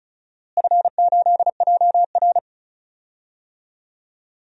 Morse code, heard as a keyed tone, about 34 wpm, 700 Hz. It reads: F8JR
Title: DXCC-100 Call Signs - 4 Characters - Repeated 3x - 35wpm